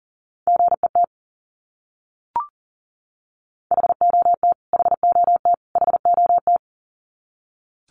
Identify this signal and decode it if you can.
Morse code, heard as a keyed tone, about 38 wpm, 700 Hz. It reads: GET HOT HOT HOT
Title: Top 200 Words Repeated 40wpm